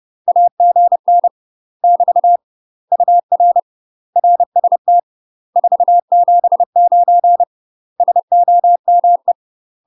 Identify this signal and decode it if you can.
Morse code, heard as a keyed tone, about 30 wpm, 700 Hz. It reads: AGN <BT> UR RST 479 SOME
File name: UK MASTER QSOs TRAINING / 30WPM / UK MASTER QSOs 0001